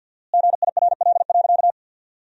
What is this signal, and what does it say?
Morse code, heard as a keyed tone, about 50 wpm, 700 Hz. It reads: GIRL.
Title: 4x Word-Spacing - Sentences from Top 400 Words - 50wpm